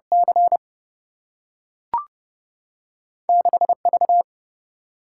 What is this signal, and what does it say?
Morse code, heard as a keyed tone, about 32 wpm, 700 Hz. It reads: C 6 V
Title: Single-Letter Single-Number 30wpm